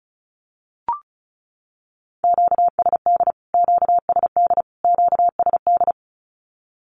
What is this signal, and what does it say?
Morse code, heard as a keyed tone, about 34 wpm, 700 Hz. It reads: QSD QSD QSD
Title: Q-Codes Repeated 3x 35wpm